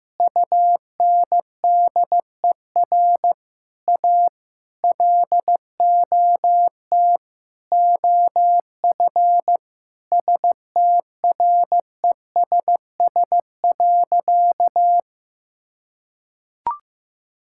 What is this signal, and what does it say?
Morse code, heard as a keyed tone, about 15 wpm, 700 Hz. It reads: UNDER A LOT OF STRESS.
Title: Sentences from Top 1000 Words - Review Part 6 - 15wpm